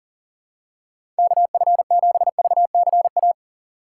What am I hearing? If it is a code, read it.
Morse code, heard as a keyed tone, about 40 wpm, 700 Hz. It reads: KF7VCA
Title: USA Call Signs - 6 Characters - 6x Character Spacing - 40wpm